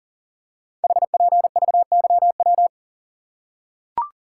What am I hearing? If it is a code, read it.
Morse code, heard as a keyed tone, about 41 wpm, 690 Hz. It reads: HPVYW
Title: Code Groups - 5 Letters - 40wpm